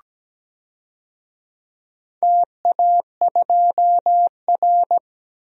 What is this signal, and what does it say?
Morse code, heard as a keyed tone, about 17 wpm, 700 Hz. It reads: TA2R
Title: DXCC-100 Call Signs - 4 Characters - 17wpm